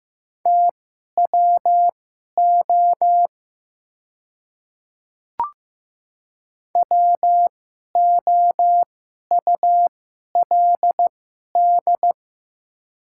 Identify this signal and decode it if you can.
Morse code, heard as a keyed tone, about 15 wpm, spaced as for 11 wpm, 700 Hz. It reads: TWO WOULD
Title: 2x Character Spacing - Top 100 Words - 15wpm